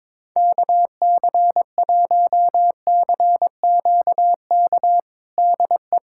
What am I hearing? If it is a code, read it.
Morse code, heard as a keyed tone, about 21 wpm, 700 Hz. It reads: KC1CQK DE
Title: USA Callsign DE Callsign 22wpm